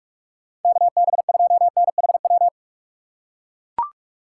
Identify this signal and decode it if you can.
Morse code, heard as a keyed tone, about 42 wpm, 700 Hz. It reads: KB2NHW
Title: USA Call Signs - 8x Character Spacing - 45wpm